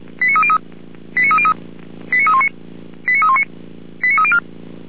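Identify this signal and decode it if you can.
Five-tone selective call (ZVEI), heard as quick runs of five tones, tones of about 70 ms. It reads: 89303 89303 89319 89319 89304
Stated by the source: ZVEI five-tone selective call received via SDR, from the Feuerwehr Orlamünde audio server.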